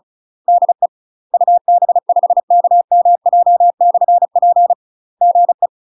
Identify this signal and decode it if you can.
Morse code, heard as a keyed tone, about 34 wpm, 700 Hz. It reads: DE UB5KMJ/P GE
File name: UK MASTER QSOs TRAINING / 35WPM / UK MASTER QSOs 0037